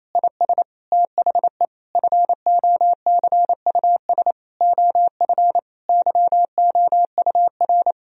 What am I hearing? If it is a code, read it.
Morse code, heard as a keyed tone, about 28 wpm, 700 Hz. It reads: IS THE FOCUS OF YOUR
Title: Sentences from Top 1000 Words - Review Part 4 - 28wpm